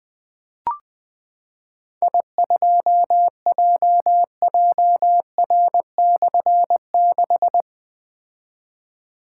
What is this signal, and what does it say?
Morse code, heard as a keyed tone, about 20 wpm, 700 Hz. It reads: I2JJR/6
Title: Difficult Call Signs 20wpm